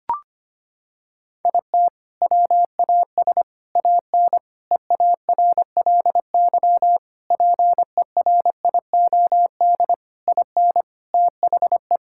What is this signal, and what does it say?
Morse code, heard as a keyed tone, about 25 wpm, 700 Hz. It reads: IT WAS AN EARLY PERIOD IN THE
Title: Sentences from Top 1000 Words - Review Part 5 - 25wpm